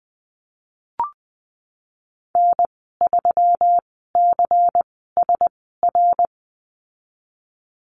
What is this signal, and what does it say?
Morse code, heard as a keyed tone, about 20 wpm, spaced as for 15 wpm, 700 Hz. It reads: N3CSR
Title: 147 Lesson 24g - Letter y - Call Signs 20wpm